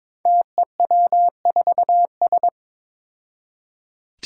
Morse code, heard as a keyed tone, about 22 wpm, 700 Hz. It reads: TEW4S
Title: Five Challenging Letter-Number Combinations 22wpm